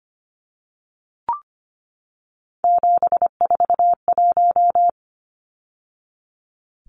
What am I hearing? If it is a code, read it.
Morse code, heard as a keyed tone, about 25 wpm, 700 Hz. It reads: 741
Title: Numbers - 3-Digit 25wpm